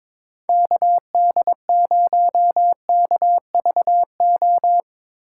Morse code, heard as a keyed tone, about 22 wpm, 700 Hz. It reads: KD0KVO
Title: USA Call Signs - 6 Characters - 8x Character Spacing - 22wpm